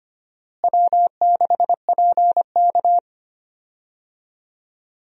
Morse code, heard as a keyed tone, about 25 wpm, 700 Hz. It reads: W6PK
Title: USA Call Signs - 4 Characters - Repeated 2x - 25wpm